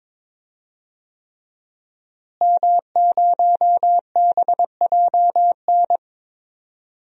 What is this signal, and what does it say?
Morse code, heard as a keyed tone, about 22 wpm, 700 Hz. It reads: M0BJN
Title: DXCC-100 Call Signs - 5 Characters - Repeated 3x - 22wpm